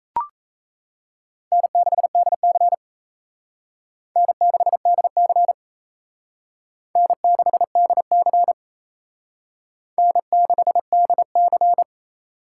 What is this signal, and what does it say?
Morse code, heard as a keyed tone, about 34 wpm, 700 Hz. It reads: N6DC N6DC N6DC N6DC
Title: Speed Racing - USA 1x2 Call Signs 28wpm